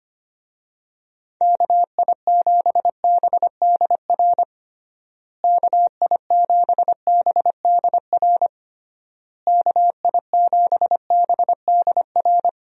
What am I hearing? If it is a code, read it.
Morse code, heard as a keyed tone, about 25 wpm, 700 Hz. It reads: KI7BDR KI7BDR KI7BDR
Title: USA Call Signs - 6 Characters - Repeated 3x - 25wpm